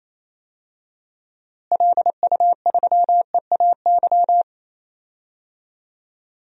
Morse code, heard as a keyed tone, about 28 wpm, 700 Hz. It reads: LU3EAY
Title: DXCC-100 Call Signs - 6 Characters - Repeated 3x - 28wpm